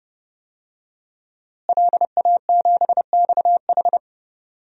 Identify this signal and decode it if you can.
Morse code, heard as a keyed tone, about 29 wpm, 700 Hz. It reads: LA7XH
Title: DXCC-100 Call Signs - 5 Characters - Repeated 3x - 30wpm